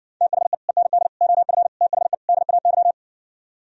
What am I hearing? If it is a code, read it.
Morse code, heard as a keyed tone, about 60 wpm, 700 Hz. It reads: THE END OF THE DAY